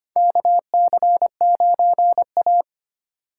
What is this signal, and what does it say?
Morse code, heard as a keyed tone, about 25 wpm, 700 Hz. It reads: KC9A
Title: USA Call Signs - 4 Characters - 2x Character Spacing - 25wpm